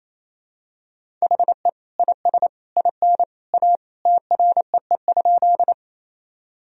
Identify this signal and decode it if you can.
Morse code, heard as a keyed tone, about 27 wpm, 700 Hz. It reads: HE IS IN A TREE?
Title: Sentences from Top 1000 Words - Review Part 1 - 28wpm